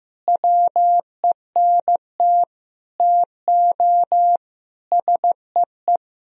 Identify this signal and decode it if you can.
Morse code, heard as a keyed tone, about 15 wpm, 700 Hz. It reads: WENT TO SEE
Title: Sentences from Top 1000 Words - Review Part 1 - 15wpm